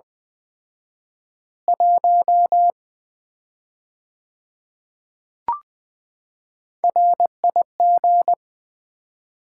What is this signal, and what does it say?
Morse code, heard as a keyed tone, about 20 wpm, 700 Hz. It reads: 1 RIG
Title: Common QSO Phrases - 4x Word-Spacing - 20wpm